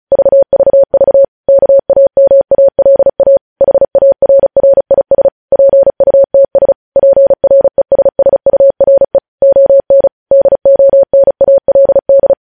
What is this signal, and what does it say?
Morse code, heard as a keyed tone, about 36 wpm, 550 Hz. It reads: VVV KAMALA HARRIS PUTS PRESSURE ON DONALD